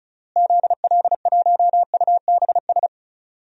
Morse code, heard as a keyed tone, about 36 wpm, 700 Hz. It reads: ZL1UBS
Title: DXCC-100 Call Signs - 6 Characters - Repeated 3x - 35wpm